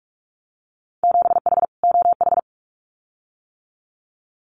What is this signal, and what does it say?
Morse code, heard as a keyed tone, about 46 wpm, 700 Hz. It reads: 7H OH
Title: Winter Field Day Exchanges Encore 45wpm